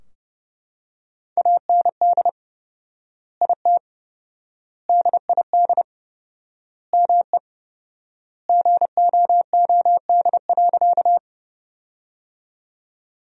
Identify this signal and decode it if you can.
Morse code, heard as a keyed tone, about 30 wpm, 700 Hz. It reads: AND IT DID ME GOOD.
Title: 4x Word-Spacing - Sentences from Top 200 Words - 30wpm